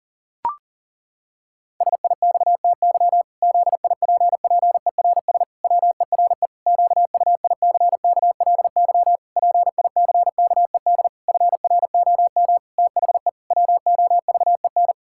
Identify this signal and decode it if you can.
Morse code, heard as a keyed tone, about 37 wpm, 700 Hz. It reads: SIXTY ZIPPERS WERE QUICKLY PICKED FROM THE WOVEN
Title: Pangram Challenge v2 - 40wpm